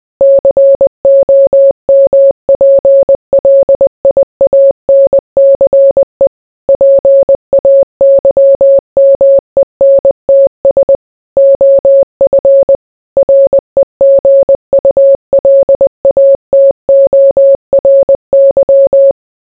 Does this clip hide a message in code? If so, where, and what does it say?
Yes: Morse code from 0.2 to 19.1 s, heard as a keyed tone, about 20 wpm, 550 Hz. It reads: COMPLIANCE PAYMENTS OF REGULATORY